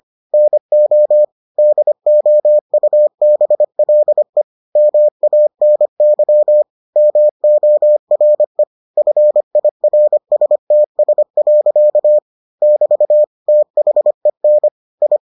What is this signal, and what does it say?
Morse code, heard as a keyed tone, about 25 wpm, 600 Hz. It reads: NO DOUBLE MANY MORE FIRSTS. <BT> THEN I